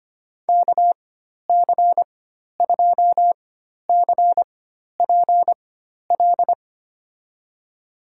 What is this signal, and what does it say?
Morse code, heard as a keyed tone, about 25 wpm, spaced as for 12 wpm, 700 Hz. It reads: KC2CPL
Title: USA Call Signs - 4x Character Spacing - 25wpm